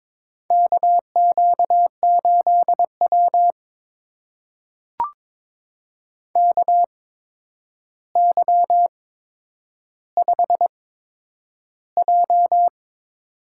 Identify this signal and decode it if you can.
Morse code, heard as a keyed tone, about 22 wpm, spaced as for 9 wpm, 700 Hz. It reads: KQ8W KY5J
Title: USA Call Signs - 4 Characters - 8x Character Spacing - 22wpm